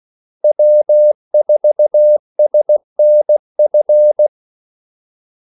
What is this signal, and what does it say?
Morse code, heard as a keyed tone, about 16 wpm, 600 Hz. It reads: W4SNF
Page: Kaldesignaler 16 wpm